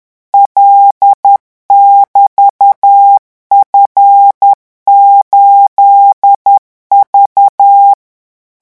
Morse code, heard as a keyed tone, about 10 wpm, 800 Hz. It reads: L<BT>F8V
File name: morse mp3 / revision13-vitesse 10 mots